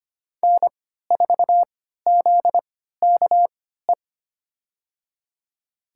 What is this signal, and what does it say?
Morse code, heard as a keyed tone, about 25 wpm, spaced as for 14 wpm, 700 Hz. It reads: N4ZKE
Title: USA Call Signs - 5 Characters - 3x Character Spacing - 25wpm